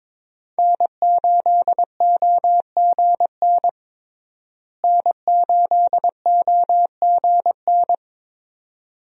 Morse code, heard as a keyed tone, about 22 wpm, 700 Hz. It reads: N8OGN N8OGN
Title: USA Call Signs - 5 Characters - Repeated 2x - 22wpm